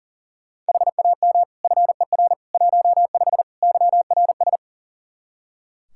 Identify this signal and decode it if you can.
Morse code, heard as a keyed tone, about 40 wpm, 700 Hz. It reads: HAM FER 15 YRS
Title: K1USN SST QSO Practice 40wpm